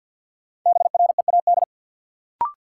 Morse code, heard as a keyed tone, about 52 wpm, 700 Hz. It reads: BREAD
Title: Sentences from Top 1600 Words - New Word Drill Alt - 50wpm